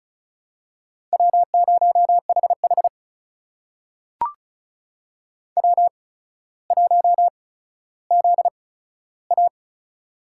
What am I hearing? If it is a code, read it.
Morse code, heard as a keyed tone, about 36 wpm, spaced as for 15 wpm, 700 Hz. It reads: W0HH W1ZA
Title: USA Call Signs - 4 Characters - 8x Character Spacing - 35wpm